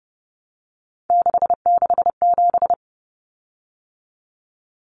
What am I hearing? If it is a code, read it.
Morse code, heard as a keyed tone, about 30 wpm, 700 Hz. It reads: 667
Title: Common Cut Numbers - 3-Digit 30wpm